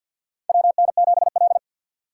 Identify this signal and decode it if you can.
Morse code, heard as a keyed tone, about 46 wpm, 700 Hz. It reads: WN7L
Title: USA Call Signs - 4 Characters - 2x Character Spacing - 50wpm